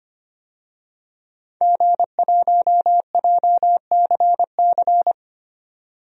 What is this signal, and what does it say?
Morse code, heard as a keyed tone, about 25 wpm, 700 Hz. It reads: G1JCC
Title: DXCC-100 Call Signs - 5 Characters - 25wpm